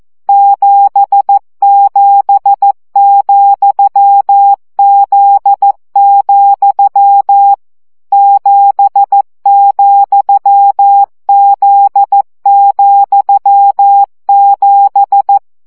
Morse code, heard as a keyed tone, about 14 wpm, 800 Hz. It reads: 77,Z, 7,Z,7
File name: CW gyakorlas z 7 vesszo 12 baud